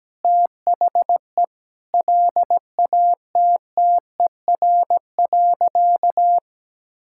Morse code, heard as a keyed tone, about 17 wpm, 700 Hz. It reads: THE LATTER.
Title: Sentences from Top 1600 Words 17wpm